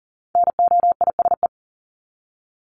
Morse code, heard as a keyed tone, about 40 wpm, 700 Hz. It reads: NOISE
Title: Top 1501-1750 Words 40wpm